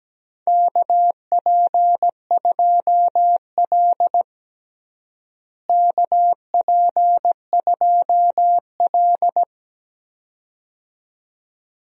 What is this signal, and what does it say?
Morse code, heard as a keyed tone, about 17 wpm, 700 Hz. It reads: KP2L KP2L
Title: DXCC-100 Call Signs - 4 Characters - Repeated 2x - 17wpm